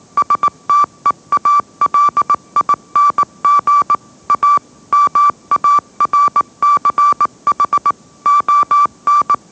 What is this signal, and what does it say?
Morse code, heard as a keyed tone, about 18 wpm, 1160 Hz. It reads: STEALING A MARCH ON
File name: mrx / mrx081 20wpm